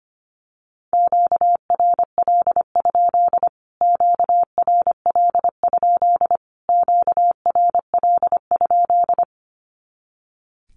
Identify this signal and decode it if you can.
Morse code, heard as a keyed tone, about 25 wpm, 700 Hz. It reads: QRL? QRL? QRL?
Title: Q-Codes Repeated 3x 25wpm